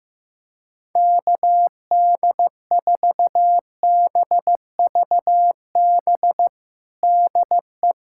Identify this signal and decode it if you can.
Morse code, heard as a keyed tone, about 15 wpm, 700 Hz. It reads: KD4BVB DE
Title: USA Callsign DE Callsign - 6 Characters - 15wpm